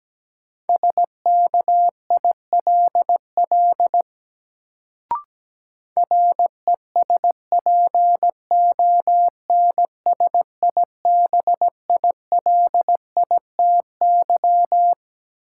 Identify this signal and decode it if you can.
Morse code, heard as a keyed tone, about 17 wpm, 700 Hz. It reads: SKILL RESPONSIBILITY
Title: Sentences from Top 1200 Words - New Word Drill - 17wpm